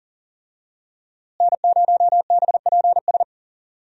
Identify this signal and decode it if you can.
Morse code, heard as a keyed tone, about 40 wpm, 700 Hz. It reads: N0BPS
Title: USA Call Signs - 5 Characters - Repeated 2x - 40wpm